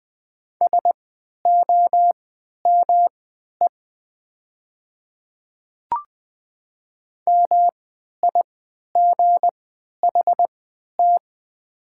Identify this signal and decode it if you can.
Morse code, heard as a keyed tone, about 21 wpm, spaced as for 12 wpm, 700 Hz. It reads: SOME MIGHT
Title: 3x Character Spacing - Top 300 Words - 20wpm